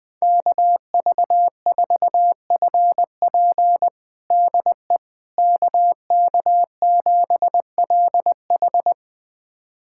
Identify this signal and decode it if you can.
Morse code, heard as a keyed tone, about 20 wpm, 700 Hz. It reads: KV4FP DE KK7LH
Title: USA Callsign DE Callsign - 5 Characters - 20wpm